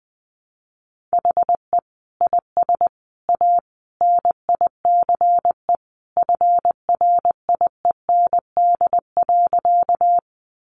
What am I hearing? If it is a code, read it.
Morse code, heard as a keyed tone, about 20 wpm, 700 Hz. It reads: HE IS A NICE FRIEND.
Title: Sentences from Top 1100 Words 20wpm